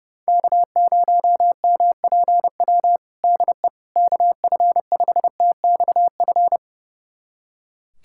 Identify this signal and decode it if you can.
Morse code, heard as a keyed tone, about 32 wpm, 690 Hz. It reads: K0MPW DE KF5TXF
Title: USA Callsign DE Callsign 30wpm